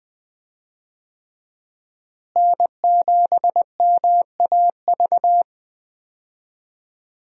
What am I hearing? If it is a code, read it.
Morse code, heard as a keyed tone, about 20 wpm, 700 Hz. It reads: N7MAV
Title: USA Call Signs - 5 Characters - Repeated 3x - 20wpm